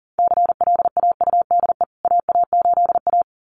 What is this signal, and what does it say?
Morse code, heard as a keyed tone, about 40 wpm, 700 Hz. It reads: CLAUDE AA8A
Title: CWT Exchange 40wpm